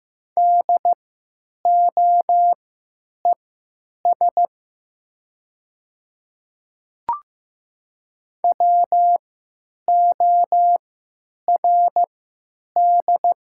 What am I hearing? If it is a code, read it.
Morse code, heard as a keyed tone, about 15 wpm, spaced as for 8 wpm, 700 Hz. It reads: DOES WORD
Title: 3x Character Spacing - Top 300 Words - 15wpm